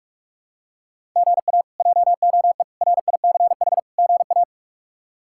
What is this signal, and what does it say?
Morse code, heard as a keyed tone, about 46 wpm, 700 Hz. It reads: GA JOE RICH GA
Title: K1USN SST QSO Practice - 45wpm